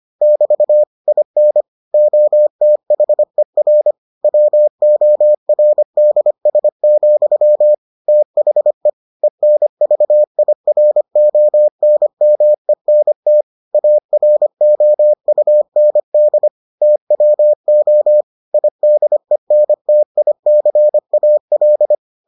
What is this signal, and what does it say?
Morse code, heard as a keyed tone, about 25 wpm, 600 Hz. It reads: <BT> IN OTHER WORDS, THE ENVIRONMENT AROUND TWO IDENTICAL